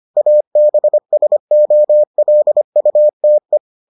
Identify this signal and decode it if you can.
Morse code, heard as a keyed tone, about 25 wpm, 600 Hz. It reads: ABSOLUTE